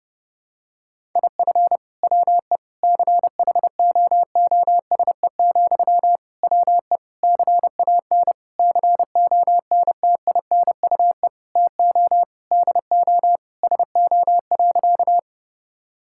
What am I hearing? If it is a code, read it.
Morse code, heard as a keyed tone, about 30 wpm, 700 Hz. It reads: IF WE CHOOSE, WE CAN CONTINUE TO DO SO.
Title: Sentences from Top 1000 Words - Review Part 6 - 30wpm